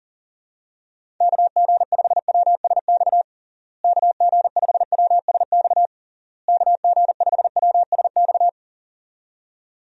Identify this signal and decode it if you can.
Morse code, heard as a keyed tone, about 40 wpm, 700 Hz. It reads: KG5WSX KG5WSX KG5WSX
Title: USA Call Signs - 6 Characters - Repeated 3x - 40wpm